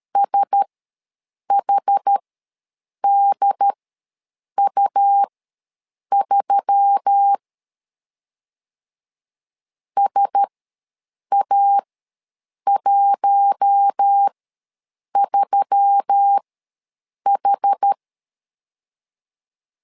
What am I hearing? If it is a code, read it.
Morse code, heard as a keyed tone, about 13 wpm, spaced as for 7 wpm, 800 Hz. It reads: SHDU3 SA13H
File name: LEZIONI / 4 Q1S3 ws 12wpm 2038